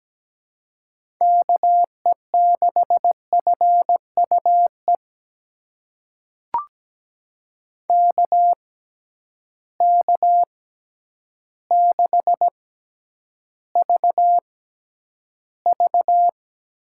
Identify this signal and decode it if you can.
Morse code, heard as a keyed tone, about 17 wpm, spaced as for 9 wpm, 700 Hz. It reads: KE6FUE KK6VV
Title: USA Call Signs - 6 Characters - 6x Character Spacing - 17wpm